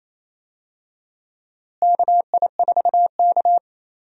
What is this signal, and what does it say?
Morse code, heard as a keyed tone, about 28 wpm, 700 Hz. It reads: KI4K
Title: USA Call Signs - 4 Characters - Repeated 3x - 28wpm